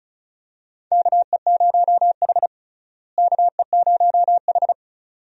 Morse code, heard as a keyed tone, about 34 wpm, 700 Hz. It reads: KE0H KE0H
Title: USA Call Signs - 4 Characters - Repeated 2x - 35wpm